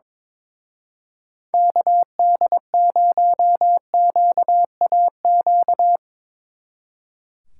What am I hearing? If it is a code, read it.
Morse code, heard as a keyed tone, about 22 wpm, 700 Hz. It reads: KD0QAQ
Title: Speed Racing - USA 2x3 Call Signs 22wpm